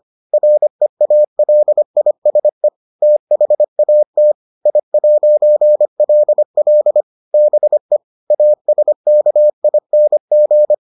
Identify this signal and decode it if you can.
Morse code, heard as a keyed tone, about 25 wpm, 600 Hz. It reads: REALISE THAT I'LL BE ASKING